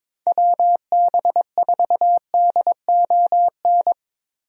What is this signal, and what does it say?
Morse code, heard as a keyed tone, about 22 wpm, 700 Hz. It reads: WB4DON
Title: USA Call Signs - 6 Characters - Repeated 2x - 22wpm